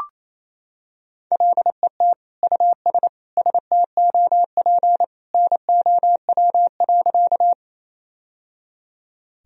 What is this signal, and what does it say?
Morse code, heard as a keyed tone, about 28 wpm, 700 Hz. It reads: LET US STOP NOW.
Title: Sentences from Top 1000 Words - Review Part 1 - 28wpm